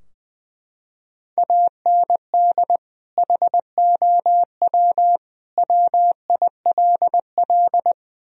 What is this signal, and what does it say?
Morse code, heard as a keyed tone, about 21 wpm, 700 Hz. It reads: AND HOW WILL
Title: Sentences from Top 1000 Words - Review Part 4 - 20wpm